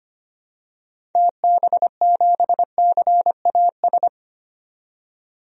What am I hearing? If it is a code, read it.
Morse code, heard as a keyed tone, about 25 wpm, 700 Hz. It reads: TB7CAS
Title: DXCC-100 Call Signs - 6 Characters - 25wpm